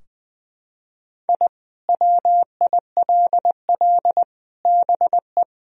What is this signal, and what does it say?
Morse code, heard as a keyed tone, about 20 wpm, 700 Hz. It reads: I WILL BE
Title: Sentences from Top 1000 Words - Review Part 2 - 20wpm